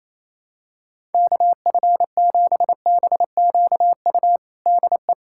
Morse code, heard as a keyed tone, about 28 wpm, 700 Hz. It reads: KF7BQU DE
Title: USA Callsign DE Callsign 28wpm